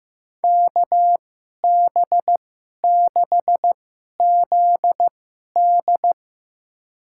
Morse code, heard as a keyed tone, about 15 wpm, spaced as for 11 wpm, 700 Hz. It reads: KB6ZD
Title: USA Call Signs - 2x Character Spacing - 15wpm